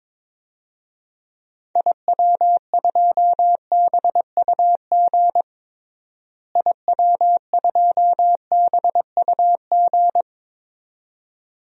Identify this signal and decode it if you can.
Morse code, heard as a keyed tone, about 22 wpm, 700 Hz. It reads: IW2BUG IW2BUG
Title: DXCC-100 Call Signs - 6 Characters - Repeated 2x - 22wpm